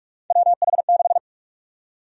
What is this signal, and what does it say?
Morse code, heard as a keyed tone, about 42 wpm, 700 Hz. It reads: WH6
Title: USA Call Sign Prefixes - 45wpm